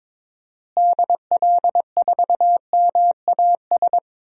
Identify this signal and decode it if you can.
Morse code, heard as a keyed tone, about 22 wpm, 700 Hz. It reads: DL4MAS
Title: DXCC-100 Call Signs - 6 Characters - Repeated 3x - 22wpm